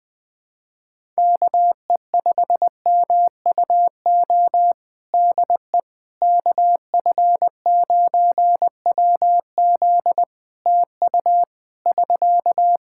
Morse code, heard as a keyed tone, about 20 wpm, 700 Hz. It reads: KE5MUO DE KF9WZ TU <SK>
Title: K1USN SST QSO Practice 20wpm